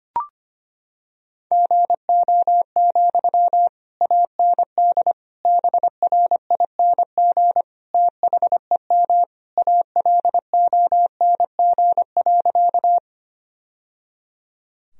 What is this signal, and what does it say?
Morse code, heard as a keyed tone, about 25 wpm, 700 Hz. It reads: GO, AND BRING THEM ALONG.
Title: Sentences from Top 1000 Words - Review Part 3 - 25wpm